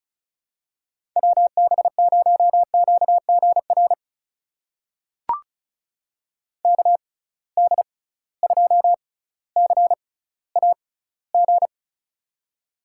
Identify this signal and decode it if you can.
Morse code, heard as a keyed tone, about 34 wpm, spaced as for 18 wpm, 700 Hz. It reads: WB0QGR KD2CAG
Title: USA Call Signs - 6x Character Spacing - 35wpm